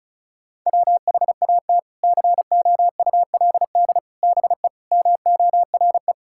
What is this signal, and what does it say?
Morse code, heard as a keyed tone, about 36 wpm, 700 Hz. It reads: WHAT COULD BE MORE
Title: Sentences from Top 1000 Words - Review Part 6 - 35wpm